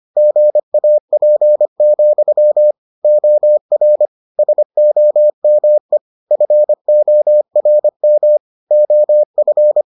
Morse code, heard as a keyed tone, about 25 wpm, 600 Hz. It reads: GAP, OR SOME FORM OF